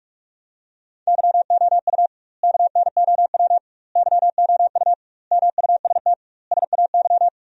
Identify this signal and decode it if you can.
Morse code, heard as a keyed tone, about 43 wpm, 700 Hz. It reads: YOU KNOW YOU MUST SAY
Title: 2x Word-Spacing - Sentences from Top 500 Words - 45wpm